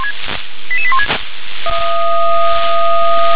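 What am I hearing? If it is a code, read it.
Five-tone selective call (ZVEI), heard as quick runs of five tones, tones of about 70 ms. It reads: 88816
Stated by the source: Five-tone ZVEI selective call received on an SDR.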